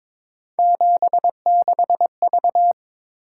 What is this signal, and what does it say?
Morse code, heard as a keyed tone, about 26 wpm, 700 Hz. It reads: 76V
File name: Speed-Racing - Three Letter-Number Combinations - 22wpm